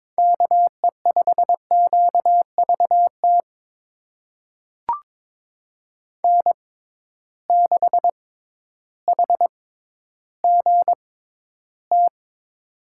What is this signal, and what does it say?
Morse code, heard as a keyed tone, about 22 wpm, spaced as for 12 wpm, 700 Hz. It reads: KE5QVT N6HGT
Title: USA Call Signs - 6x Character Spacing - 22wpm